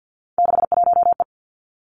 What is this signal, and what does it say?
Morse code, heard as a keyed tone, about 50 wpm, 700 Hz. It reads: 61E
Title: Three Challenging Letter-Number Combinations 50wpm